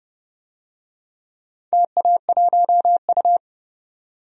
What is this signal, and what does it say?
Morse code, heard as a keyed tone, about 30 wpm, 700 Hz. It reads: TA1U
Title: DXCC-100 Call Signs - 4 Characters - 30wpm